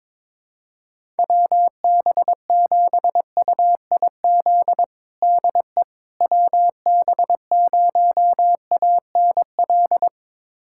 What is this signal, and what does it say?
Morse code, heard as a keyed tone, about 22 wpm, 700 Hz. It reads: WB7UIZ DE WB0ANL
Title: USA Callsign DE Callsign - 6 Characters - 22wpm